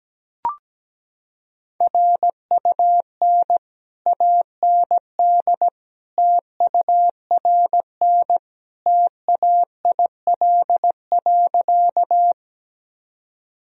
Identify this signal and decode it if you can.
Morse code, heard as a keyed tone, about 17 wpm, 700 Hz. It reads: RUN AND TURN TAIL.
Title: Sentences from Top 1000 Words - Review Part 3 - 17wpm